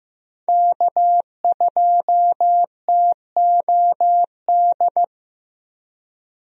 Morse code, heard as a keyed tone, about 15 wpm, 700 Hz. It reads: K2TOD
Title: USA Call Signs - 5 Characters - Repeated 2x - 15wpm